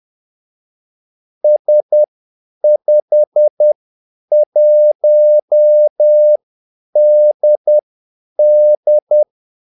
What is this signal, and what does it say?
Morse code, heard as a keyed tone, about 10 wpm, spaced as for 8 wpm, 600 Hz. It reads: S51DD
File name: Callsigns-10-8wpm